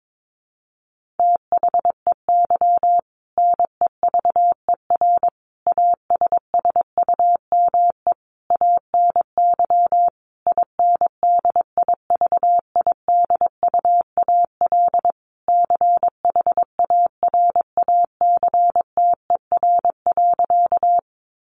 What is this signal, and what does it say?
Morse code, heard as a keyed tone, about 22 wpm, 700 Hz. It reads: THEY NEVER ASSUME ANY INDIVIDUAL CHARACTER.
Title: Sentences from Top 1500 Words 22wpm